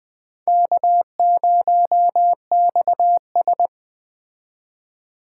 Morse code, heard as a keyed tone, about 20 wpm, 700 Hz. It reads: K0XS
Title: Speed Racing - USA 1x2 Call Signs 17wpm